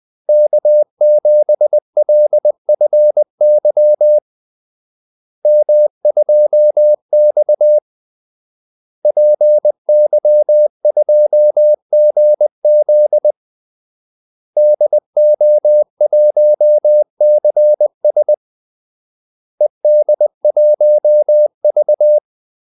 Morse code, heard as a keyed tone, about 20 wpm, 600 Hz. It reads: K7LFY M2X PY2GZ DO1CS ED1V
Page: Kaldesignaler 20 wpm